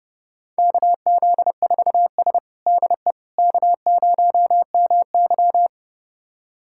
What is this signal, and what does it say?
Morse code, heard as a keyed tone, about 32 wpm, 700 Hz. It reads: KZ4S DE K0MY
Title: USA Callsign DE Callsign - 4 Characters - 30wpm